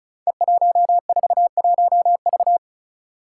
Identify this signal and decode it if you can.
Morse code, heard as a keyed tone, about 36 wpm, 700 Hz. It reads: E141V
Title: Five Challenging Letter-Number Combinations 35wpm